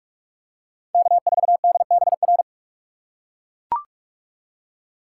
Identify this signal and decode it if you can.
Morse code, heard as a keyed tone, about 42 wpm, 700 Hz. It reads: K4DBR